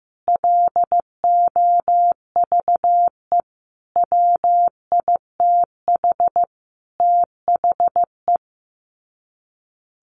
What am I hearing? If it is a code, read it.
Morse code, heard as a keyed tone, about 15 wpm, 700 Hz. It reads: LOVE WITH THE
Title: Sets of 5 Words Encore2 15wpm